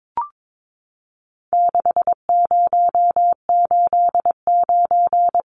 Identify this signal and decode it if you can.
Morse code, heard as a keyed tone, about 22 wpm, 700 Hz. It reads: 6089
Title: Numbers - 4-Digit 22wpm